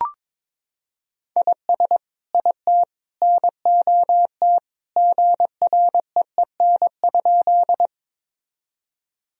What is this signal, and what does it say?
Morse code, heard as a keyed tone, about 22 wpm, 700 Hz. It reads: IS IT NOT GREEN?
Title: Sentences from Top 1000 Words - Review Part 2 - 22wpm